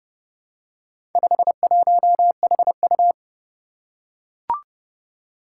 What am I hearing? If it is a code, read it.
Morse code, heard as a keyed tone, about 30 wpm, 700 Hz. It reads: 51HU
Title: Four Challenging Letter-Number Combinations 30wpm